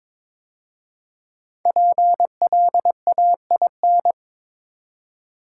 Morse code, heard as a keyed tone, about 23 wpm, 700 Hz. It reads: PLAIN
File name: 5 Letter Words - First 3 Letters Given - 22wpm